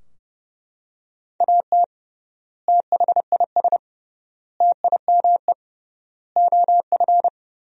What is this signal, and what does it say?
Morse code, heard as a keyed tone, about 32 wpm, 700 Hz. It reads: AT THIS TIME OF
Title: 3x Word-Spacing - Sentences from Top 300 Words - 30wpm